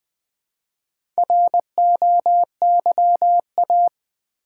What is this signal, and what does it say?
Morse code, heard as a keyed tone, about 20 wpm, 700 Hz. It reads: ROYA
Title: Build A Word - 1 Letter At A Time - 5-Letter Words - 20wpm